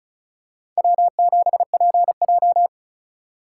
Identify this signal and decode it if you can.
Morse code, heard as a keyed tone, about 36 wpm, 700 Hz. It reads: W7PJ
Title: USA Call Signs - 3x Character Spacing - 35wpm